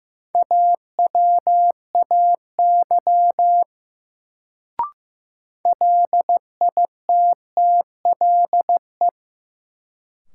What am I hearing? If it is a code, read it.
Morse code, heard as a keyed tone, about 15 wpm, 700 Hz. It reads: AWAY LITTLE
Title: Sets of 2 Words - New Word Drill - 15wpm